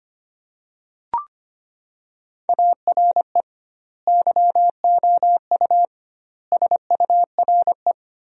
Morse code, heard as a keyed tone, about 25 wpm, 700 Hz. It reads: ARE YOU SURE
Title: 2x Word-Spacing - Sentences from Top 300 Words - 25wpm